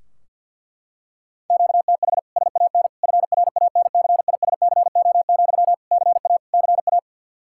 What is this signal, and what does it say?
Morse code, heard as a keyed tone, about 46 wpm, 700 Hz. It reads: QTH SAN FRANCISCO, CA CA
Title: K1USN SST QSO Practice 50wpm